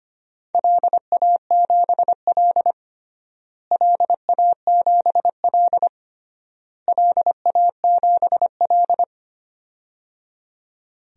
Morse code, heard as a keyed tone, about 25 wpm, 700 Hz. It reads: LA7L LA7L LA7L
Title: DXCC-100 Call Signs - 4 Characters - Repeated 3x - 25wpm